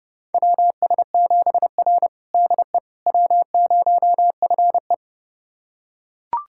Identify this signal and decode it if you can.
Morse code, heard as a keyed tone, about 30 wpm, 700 Hz. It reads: WS7R DE W0FE
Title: USA Callsign DE Callsign - 4 Characters - 30wpm